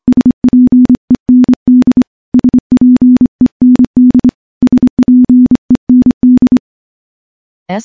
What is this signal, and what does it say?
Morse code, heard as a keyed tone, about 25 wpm, 250 Hz. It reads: SPEND SPEND SPEND
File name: morse-training-25WPM